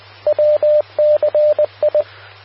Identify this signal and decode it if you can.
Morse code, heard as a keyed tone, about 20 wpm, 600 Hz, with radio noise. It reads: WCI
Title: Austin, TX Amateur Repeaters